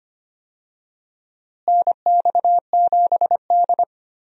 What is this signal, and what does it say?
Morse code, heard as a keyed tone, about 25 wpm, 700 Hz. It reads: NX7D
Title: USA Call Signs - 4 Characters - Repeated 2x - 25wpm